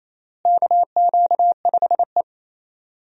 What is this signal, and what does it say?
Morse code, heard as a keyed tone, about 28 wpm, 700 Hz. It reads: KQ5E
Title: USA Call Signs - 4 Characters - Repeated 3x - 28wpm